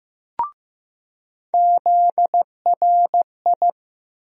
Morse code, heard as a keyed tone, about 15 wpm, 700 Hz. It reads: ZRI
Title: USA Call Sign Postfixes - 3 Letters 15wpm